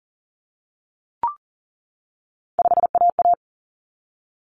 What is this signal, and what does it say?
Morse code, heard as a keyed tone, about 40 wpm, 700 Hz. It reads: 5AA
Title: Common Cut Numbers - 3-Digit 40wpm